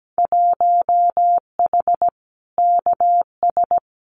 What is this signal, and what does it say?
Morse code, heard as a keyed tone, about 17 wpm, 700 Hz. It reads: 1H KS
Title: Winter Field Day Exchanges Encore 17wpm